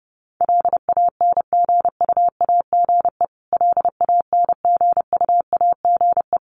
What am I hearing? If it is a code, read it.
Morse code, heard as a keyed tone, about 28 wpm, 700 Hz. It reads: LANGUAGE LANGUAGE
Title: Top 500 Words Repeated 30wpm